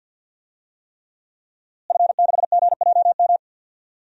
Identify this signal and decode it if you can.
Morse code, heard as a keyed tone, about 47 wpm, 700 Hz. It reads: F6GJM
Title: DXCC-100 Call Signs - 5 Characters - Repeated 3x - 50wpm